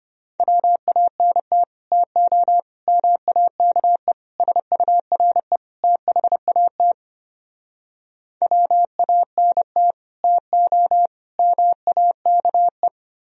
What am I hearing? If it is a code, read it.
Morse code, heard as a keyed tone, about 33 wpm, 700 Hz. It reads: WANT TO MAKE SURE THAT WANT TO MAKE